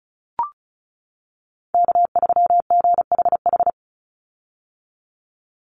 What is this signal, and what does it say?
Morse code, heard as a keyed tone, about 34 wpm, 700 Hz. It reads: K3GHH
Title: Call Signs - 35wpm